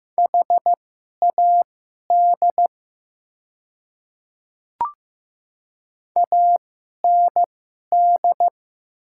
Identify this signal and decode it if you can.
Morse code, heard as a keyed tone, about 15 wpm, spaced as for 11 wpm, 700 Hz. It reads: HAD AND
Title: 2x Character Spacing - Top 100 Words - 15wpm